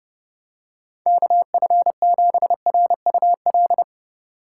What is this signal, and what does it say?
Morse code, heard as a keyed tone, about 30 wpm, 700 Hz. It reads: KF7RUL
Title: USA Call Signs - 6 Characters - 4x Character Spacing - 30wpm